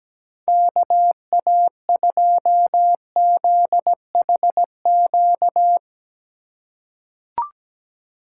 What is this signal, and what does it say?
Morse code, heard as a keyed tone, about 17 wpm, 700 Hz. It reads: KA2ZHQ
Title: USA Call Signs - 4x Character Spacing - 17wpm